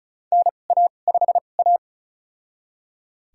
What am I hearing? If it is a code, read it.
Morse code, heard as a keyed tone, about 36 wpm, spaced as for 25 wpm, 700 Hz. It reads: NA5A
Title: USA Call Signs - 4 Characters - 2x Character Spacing - 35wpm